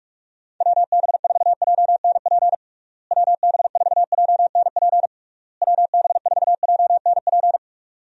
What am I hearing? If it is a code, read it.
Morse code, heard as a keyed tone, about 46 wpm, 700 Hz. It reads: WB4JNP WB4JNP WB4JNP
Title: USA Call Signs - 6 Characters - Repeated 3x - 45wpm